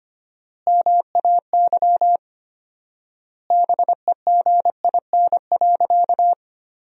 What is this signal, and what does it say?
Morse code, heard as a keyed tone, about 25 wpm, 700 Hz. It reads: MAY BEGIN.
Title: 4x Word-Spacing - Sentences from Top 300 Words - 25wpm